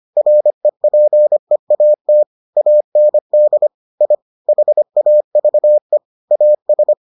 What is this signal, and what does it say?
Morse code, heard as a keyed tone, about 25 wpm, 600 Hz. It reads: REPEAT AND I HAVE AS